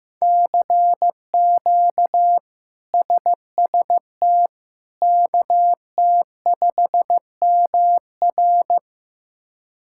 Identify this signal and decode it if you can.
Morse code, heard as a keyed tone, about 15 wpm, 700 Hz. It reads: CQ SST KT5MR
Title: K1USN SST QSO Practice - 15wpm